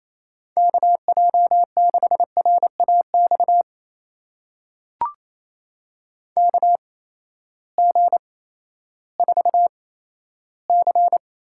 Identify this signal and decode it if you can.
Morse code, heard as a keyed tone, about 28 wpm, spaced as for 14 wpm, 700 Hz. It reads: KJ6RAX KG4C
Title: USA Call Signs - 6 Characters - 8x Character Spacing - 28wpm